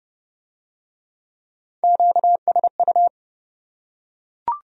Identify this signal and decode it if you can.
Morse code, heard as a keyed tone, about 29 wpm, 700 Hz. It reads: QSU